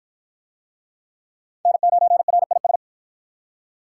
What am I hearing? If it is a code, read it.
Morse code, heard as a keyed tone, about 48 wpm, 700 Hz. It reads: N9RIS